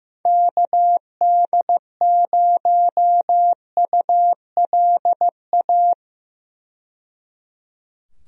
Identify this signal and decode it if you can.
Morse code, heard as a keyed tone, about 15 wpm, 700 Hz. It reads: KD0ULA
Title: USA Call Signs - 6 Characters - 15wpm